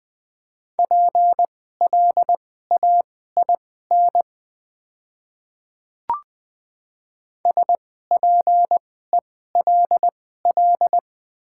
Morse code, heard as a keyed tone, about 21 wpm, spaced as for 15 wpm, 690 Hz. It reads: PLAIN SPELL
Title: 2x Character Spacing - Top 500 Words - 20wpm